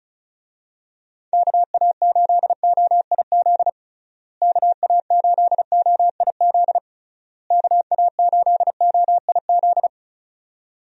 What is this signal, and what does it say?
Morse code, heard as a keyed tone, about 36 wpm, 700 Hz. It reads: KA8OIZ KA8OIZ KA8OIZ
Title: USA Call Signs - 6 Characters - Repeated 3x - 35wpm